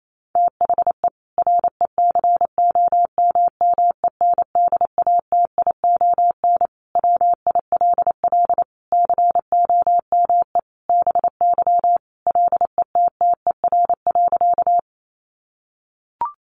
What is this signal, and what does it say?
Morse code, heard as a keyed tone, about 28 wpm, 700 Hz. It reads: THE RECOMMENDATION WILL COME BY LETTER.
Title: Sentences from Top 1400 Words 28wpm